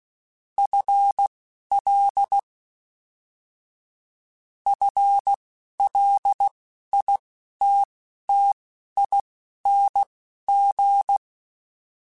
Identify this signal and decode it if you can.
Morse code, heard as a keyed tone, about 16 wpm, spaced as for 11 wpm, 800 Hz. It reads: FL FLITTING